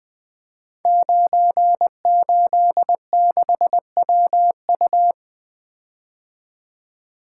Morse code, heard as a keyed tone, about 20 wpm, 700 Hz. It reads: 986WU
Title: Code Groups - 5 Letter:Numbers - 20wpm